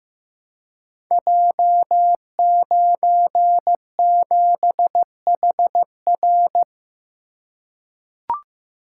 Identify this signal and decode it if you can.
Morse code, heard as a keyed tone, about 15 wpm, 700 Hz. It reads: J97HR
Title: Code Groups - 5 Letter:Numbers - 15wpm